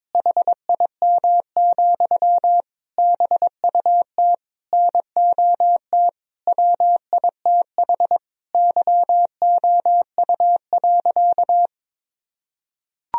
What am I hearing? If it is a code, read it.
Morse code, heard as a keyed tone, about 21 wpm, 700 Hz. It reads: HIM, BUT NOT WITH YOU.
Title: Sentences from Top 1000 Words - Review Part 4 - 22wpm